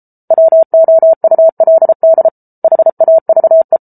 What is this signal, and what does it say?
Morse code, heard as a keyed tone, about 34 wpm, 650 Hz. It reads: WOULD HAVE